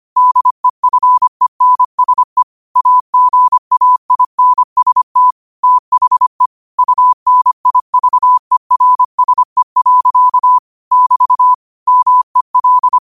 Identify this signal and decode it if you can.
Morse code, heard as a keyed tone, about 25 wpm, 1000 Hz. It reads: DEFENSE AGAINST THE UNIVERSE. <BT> MEL